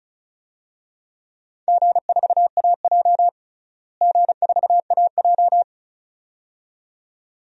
Morse code, heard as a keyed tone, about 36 wpm, 700 Hz. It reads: G4AJ G4AJ
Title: DXCC-100 Call Signs - 4 Characters - Repeated 2x - 35wpm